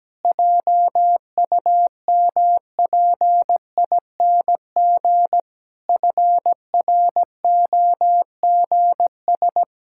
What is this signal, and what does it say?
Morse code, heard as a keyed tone, about 17 wpm, 700 Hz. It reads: JUMPING FROGS
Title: Pangram Challenge v2 - 17wpm